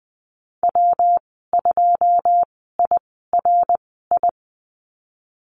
Morse code, heard as a keyed tone, about 20 wpm, spaced as for 15 wpm, 700 Hz. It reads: W2IRI
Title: 122 Lesson 20g - Number 3 - Call Signs 20wpm